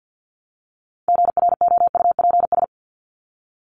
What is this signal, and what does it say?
Morse code, heard as a keyed tone, about 50 wpm, 700 Hz. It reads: GROUPS
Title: ES:S Suffix - Top 100 Words with Base Words 50wpm